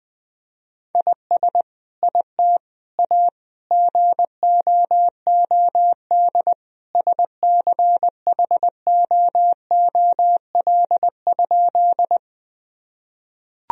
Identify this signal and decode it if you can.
Morse code, heard as a keyed tone, about 20 wpm, 700 Hz. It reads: IS IT A GOOD SCHOOL?
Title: Sentences from Top 1000 Words - Review Part 1 - 20wpm